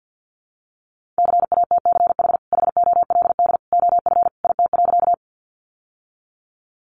Morse code, heard as a keyed tone, about 50 wpm, 700 Hz. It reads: CATCH HOLD OF IT.
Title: Sentences from Top 1500 Words 50wpm